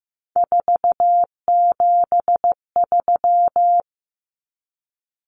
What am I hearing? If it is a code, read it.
Morse code, heard as a keyed tone, about 15 wpm, 700 Hz. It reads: 473
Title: Numbers - 3-Digit 15wpm